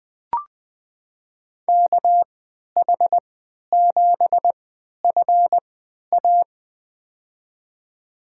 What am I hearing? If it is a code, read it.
Morse code, heard as a keyed tone, about 20 wpm, spaced as for 11 wpm, 700 Hz. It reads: KH7FA
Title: USA Call Signs - 5 Characters - 3x Character Spacing - 20wpm